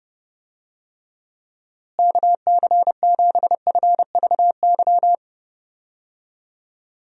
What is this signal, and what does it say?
Morse code, heard as a keyed tone, about 32 wpm, 700 Hz. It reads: KC7FVY
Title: USA Call Signs - 6 Characters - Repeated 2x - 30wpm